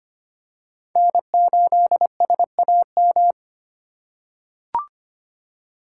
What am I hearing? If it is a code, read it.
Morse code, heard as a keyed tone, about 25 wpm, 700 Hz. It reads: N8SAM
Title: USA Call Signs - 5 Characters - 3x Character Spacing - 25wpm